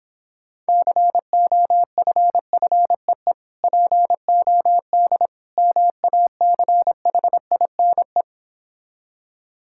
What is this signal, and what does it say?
Morse code, heard as a keyed tone, about 26 wpm, 700 Hz. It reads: COFFEE POD MACHINE